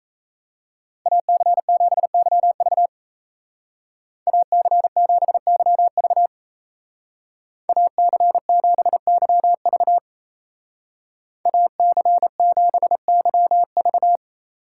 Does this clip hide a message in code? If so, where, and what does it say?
Yes: Morse code from 1.1 to 14.2 s, heard as a keyed tone, about 36 wpm, 700 Hz. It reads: AC7YV AC7YV AC7YV AC7YV